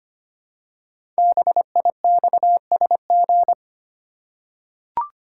Morse code, heard as a keyed tone, about 25 wpm, 700 Hz. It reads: BIXSG
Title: Code Groups - 5 Letters - 25wpm